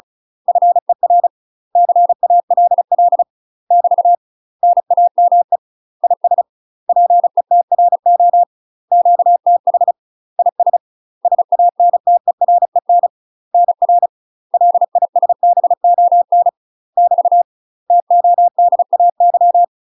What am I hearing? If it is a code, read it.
Morse code, heard as a keyed tone, about 34 wpm, 700 Hz. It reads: FER CALL <BT> NAME IS PETRO QTH IS SANTEREN NR LISBON <BT> TODAY
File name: UK MASTER QSOs TRAINING / 35WPM / UK MASTER QSOs 0006